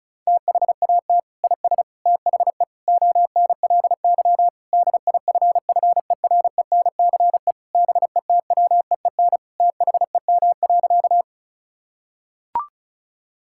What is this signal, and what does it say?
Morse code, heard as a keyed tone, about 34 wpm, 700 Hz. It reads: THAT IS THE ONLY DIFFERENCE BETWEEN THEM.
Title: Sentences from Top 1000 Words - Review Part 5 - 35wpm